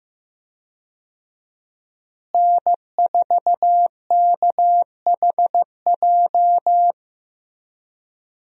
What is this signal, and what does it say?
Morse code, heard as a keyed tone, about 15 wpm, 700 Hz. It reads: N4KHJ
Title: USA Call Signs - 5 Characters - Repeated 2x - 15wpm